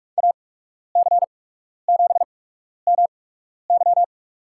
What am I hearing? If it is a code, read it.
Morse code, heard as a keyed tone, about 41 wpm, spaced as for 12 wpm, 700 Hz. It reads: AC7MY
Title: USA Call Signs - 5 Characters - 8x Character Spacing - 45wpm